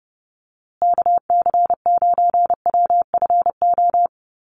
Morse code, heard as a keyed tone, about 30 wpm, 700 Hz. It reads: KC9WFO
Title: USA Call Signs Repeated 3x 30wpm